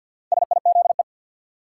Morse code, heard as a keyed tone, about 51 wpm, 700 Hz. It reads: SIZE
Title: Sentences from Top 400 Words - New Word Drill - 50wpm